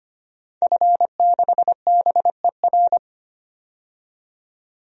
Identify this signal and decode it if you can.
Morse code, heard as a keyed tone, about 25 wpm, 700 Hz. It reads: F6BER
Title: DXCC-100 Call Signs - 5 Characters - 25wpm